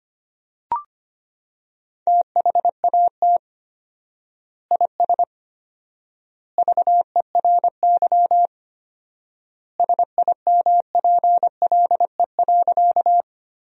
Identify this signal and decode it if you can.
Morse code, heard as a keyed tone, about 25 wpm, 700 Hz. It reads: THAT IS VERY SIMPLE.
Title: 4x Word-Spacing - Sentences from Top 400 Words - 25wpm